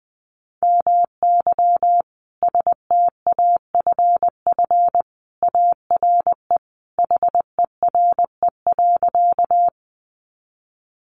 Morse code, heard as a keyed tone, about 20 wpm, 700 Hz. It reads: MY STAFF ARE HERE.
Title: Sentences from Top 1200 Words 20wpm